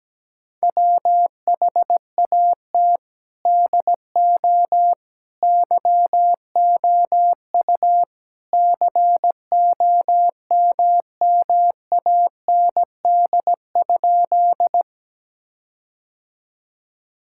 Morse code, heard as a keyed tone, about 17 wpm, 700 Hz. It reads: WHAT DO YOU COMMAND?
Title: Sentences from Top 1600 Words 17wpm